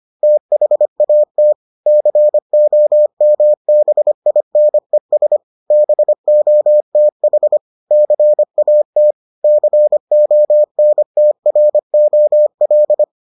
Morse code, heard as a keyed tone, about 25 wpm, 600 Hz. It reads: THAT COMBINES BOTH CAT CONTROL